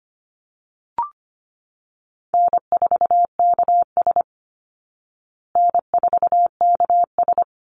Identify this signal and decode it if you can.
Morse code, heard as a keyed tone, about 25 wpm, 700 Hz. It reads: N4KS N4KS
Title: Easy Call Signs Repeated 2x - 25wpm